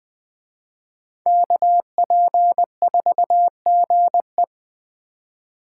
Morse code, heard as a keyed tone, about 21 wpm, 700 Hz. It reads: KP4GE
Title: Speed Racing - USA 2x2 Call Signs 17wpm